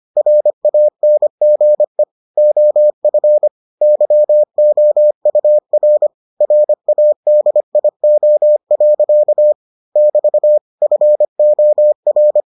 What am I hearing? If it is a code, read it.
Morse code, heard as a keyed tone, about 25 wpm, 600 Hz. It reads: RANGE OF YOUR RADIO. <BT> FOR